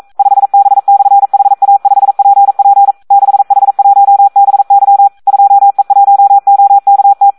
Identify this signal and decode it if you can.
Morse code, heard as a keyed tone, about 42 wpm, 800 Hz. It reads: 5B/HA5PP 6H1BY 2E1OKT
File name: Audio practica CW / SD 1 42